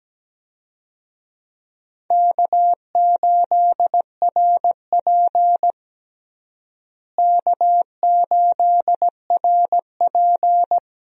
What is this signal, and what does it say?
Morse code, heard as a keyed tone, about 17 wpm, 700 Hz. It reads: K8RP K8RP
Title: USA Call Signs - 4 Characters - Repeated 3x - 17wpm